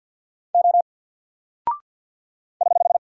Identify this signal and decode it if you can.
Morse code, heard as a keyed tone, about 50 wpm, 700 Hz. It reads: O <HH>
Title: Single Character (Letter:Number:Punctuation:Prosign) 50wpm